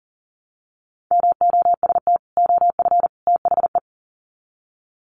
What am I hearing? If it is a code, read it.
Morse code, heard as a keyed tone, about 40 wpm, 700 Hz. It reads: MOST OF THE
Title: Sets of 3 Words 40wpm